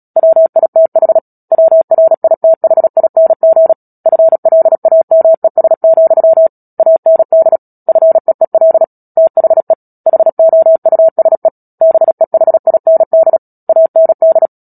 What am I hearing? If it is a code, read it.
Morse code, heard as a keyed tone, about 37 wpm, 650 Hz. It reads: WITH WRITHING FLAMES, AND FEEL THE HOUSE BEHIND AND
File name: Book Two / 10-Chapter 10 - Epilogue - 36 WPM